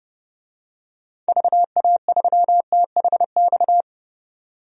Morse code, heard as a keyed tone, about 32 wpm, 700 Hz. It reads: VA3THX
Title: DXCC-100 Call Signs - 6 Characters - 30wpm